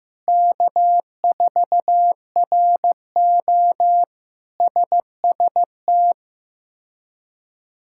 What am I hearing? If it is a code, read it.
Morse code, heard as a keyed tone, about 15 wpm, 700 Hz. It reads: K4RO SST
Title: K1USN SST QSO Practice - 15wpm